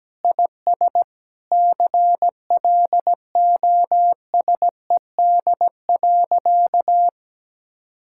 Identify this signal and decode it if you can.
Morse code, heard as a keyed tone, about 17 wpm, 700 Hz. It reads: IS CLOSED.
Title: Sentences from Top 1600 Words 17wpm